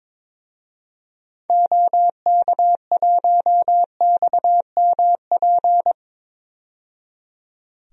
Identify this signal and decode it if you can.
Morse code, heard as a keyed tone, about 22 wpm, 700 Hz. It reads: OK1XMP
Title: DXCC-100 Call Signs - 6 Characters - 22wpm